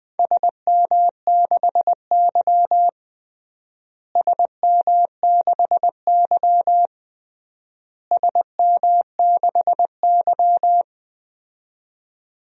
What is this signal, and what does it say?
Morse code, heard as a keyed tone, about 20 wpm, 700 Hz. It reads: SM6Y SM6Y SM6Y
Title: DXCC-100 Call Signs - 4 Characters - Repeated 3x - 20wpm